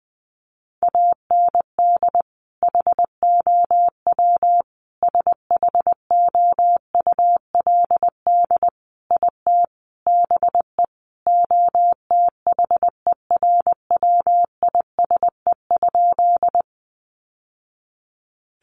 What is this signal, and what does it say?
Morse code, heard as a keyed tone, about 20 wpm, 700 Hz. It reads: AND HOW SHOULD IT BE OTHERWISE?
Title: Sentences from Top 1200 Words 20wpm